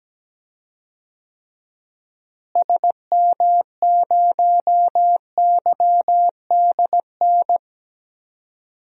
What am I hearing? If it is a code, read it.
Morse code, heard as a keyed tone, about 17 wpm, 700 Hz. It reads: SM0YDN
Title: DXCC-100 Call Signs - 6 Characters - Repeated 2x - 17wpm